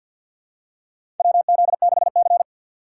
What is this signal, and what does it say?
Morse code, heard as a keyed tone, about 50 wpm, 700 Hz. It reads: WZ6C
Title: USA Call Signs - 4 Characters - 2x Character Spacing - 50wpm